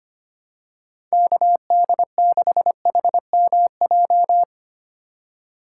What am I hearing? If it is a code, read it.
Morse code, heard as a keyed tone, about 25 wpm, 700 Hz. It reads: KD6HMJ
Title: USA Call Signs - 6 Characters - 4x Character Spacing - 25wpm